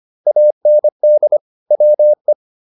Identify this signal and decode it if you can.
Morse code, heard as a keyed tone, about 25 wpm, 600 Hz. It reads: AND WE